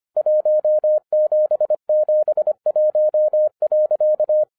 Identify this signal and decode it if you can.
Morse code, heard as a keyed tone, about 24 wpm, 610 Hz. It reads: 1771.
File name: 2012-01-mp3 / n06-FIRST FISTS CONTACT-25wpm